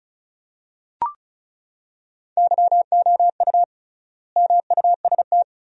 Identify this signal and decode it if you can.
Morse code, heard as a keyed tone, about 36 wpm, 700 Hz. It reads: YOU MUST
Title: 3x Word-Spacing - Sentences from Top 400 Words - 35wpm